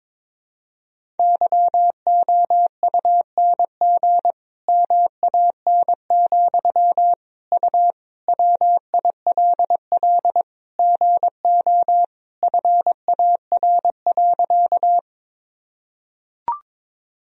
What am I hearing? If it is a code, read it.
Morse code, heard as a keyed tone, about 22 wpm, 700 Hz. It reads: YOUNG MAN, U WILL GO FAR.
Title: Sentences from Top 400 Words and Abbreviations 22wpm